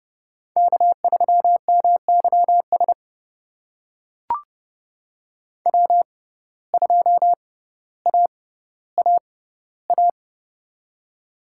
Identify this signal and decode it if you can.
Morse code, heard as a keyed tone, about 29 wpm, spaced as for 15 wpm, 700 Hz. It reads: K3MYS W2AAA
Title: USA Call Signs - 5 Characters - 6x Character Spacing - 30wpm